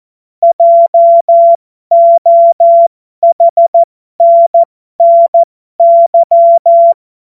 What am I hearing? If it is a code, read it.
Morse code, heard as a keyed tone, about 14 wpm, spaced as for 12 wpm, 680 Hz. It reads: JOHNNY
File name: Morsekurs / Witze